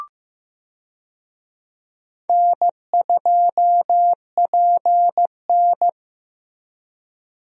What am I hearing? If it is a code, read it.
Morse code, heard as a keyed tone, about 14 wpm, 700 Hz. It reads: N2PN
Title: USA Call Signs - 4 Characters - Repeated 3x - 15wpm